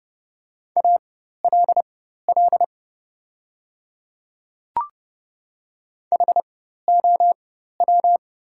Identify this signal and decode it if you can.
Morse code, heard as a keyed tone, about 30 wpm, spaced as for 14 wpm, 700 Hz. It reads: ALL HOW
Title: 4x Character Spacing - Top 100 Words - 30wpm